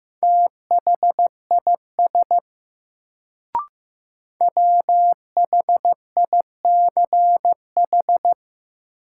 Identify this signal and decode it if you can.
Morse code, heard as a keyed tone, about 15 wpm, 700 Hz. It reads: THIS WHICH
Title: Sentences from Top 100 Words - New Word Drill - 15wpm